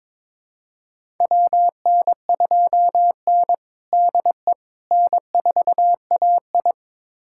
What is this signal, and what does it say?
Morse code, heard as a keyed tone, about 22 wpm, 700 Hz. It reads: WN2N DE N4AI
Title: USA Callsign DE Callsign - 4 Characters - 22wpm